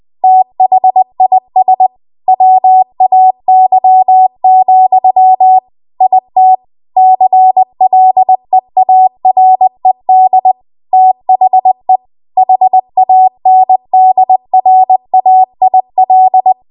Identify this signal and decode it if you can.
Morse code, heard as a keyed tone, about 20 wpm, 750 Hz. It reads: THIS WAY, IT CLEARED THE HANDRAIL
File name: Archive / 20 WPM / 20WPM